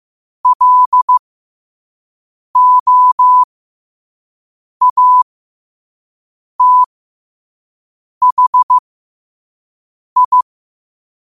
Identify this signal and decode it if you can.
Morse code, heard as a keyed tone, about 15 wpm, spaced as for 5 wpm, 1000 Hz. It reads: LOATHI
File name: qotdfiles / 5wpm